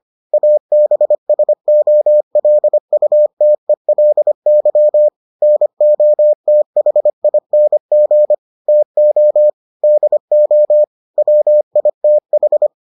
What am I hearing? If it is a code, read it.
Morse code, heard as a keyed tone, about 25 wpm, 600 Hz. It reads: ABSOLUTELY NOTHING TO DO WITH